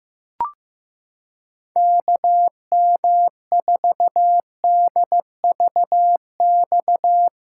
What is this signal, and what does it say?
Morse code, heard as a keyed tone, about 15 wpm, 700 Hz. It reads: KM4DVX
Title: USA Callsign DE Callsign 15wpm